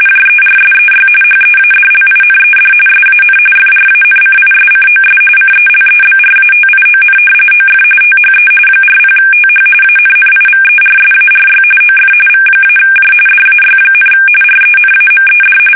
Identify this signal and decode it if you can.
RTTY, about 45.45 baud. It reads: JERSEY HEADLINED SAN FRANCISCOS CELEBRATION OF THE 40TH ANNIVERSARY OF V-J DAY, RENAMED
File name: fsk850 / battleshipnj